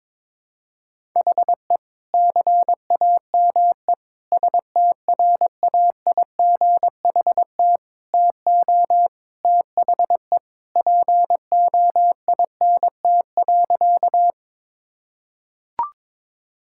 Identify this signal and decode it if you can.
Morse code, heard as a keyed tone, about 22 wpm, 700 Hz. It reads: HE CAME STRAIGHT TO THE POINT.
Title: Sentences from Top 1000 Words - Review Part 6 - 22wpm